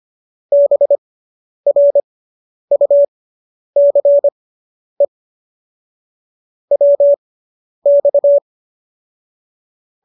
Morse code, heard as a keyed tone, about 25 wpm, spaced as for 10 wpm, 570 Hz. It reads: BRUCE WX